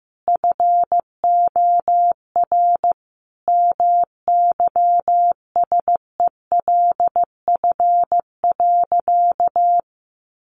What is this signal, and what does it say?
Morse code, heard as a keyed tone, about 14 wpm, 700 Hz. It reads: FOR MYSELF.
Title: Sentences from Top 1300 Words 15wpm